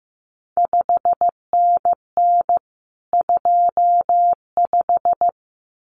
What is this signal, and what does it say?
Morse code, heard as a keyed tone, about 15 wpm, 700 Hz. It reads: 5NN 25
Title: CQ WW DX Contest Exchange - 15wpm